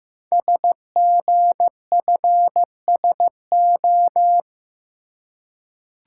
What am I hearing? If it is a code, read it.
Morse code, heard as a keyed tone, about 15 wpm, 690 Hz. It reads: SGFSO
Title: Code Groups - 5 Letters - 15wpm